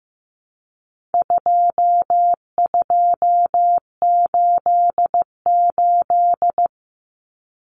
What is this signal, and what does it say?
Morse code, heard as a keyed tone, about 15 wpm, 700 Hz. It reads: 2288
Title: Common Cut Numbers - 4-Digit 15wpm